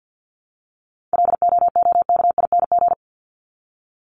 Morse code, heard as a keyed tone, about 49 wpm, 700 Hz. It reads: LOOKING
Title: ING Suffix - Top 100 Words with Base Words 50wpm